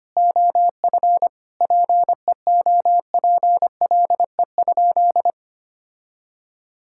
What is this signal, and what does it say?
Morse code, heard as a keyed tone, about 25 wpm, 700 Hz. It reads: OF PEOPLE?
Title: Sentences from Top 1000 Words - Review Part 6 - 25wpm